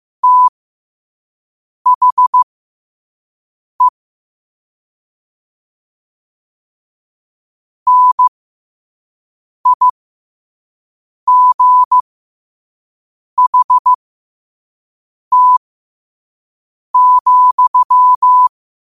Morse code, heard as a keyed tone, about 15 wpm, spaced as for 5 wpm, 1000 Hz. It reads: THE NIGHT,
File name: qotdfiles / 5wpm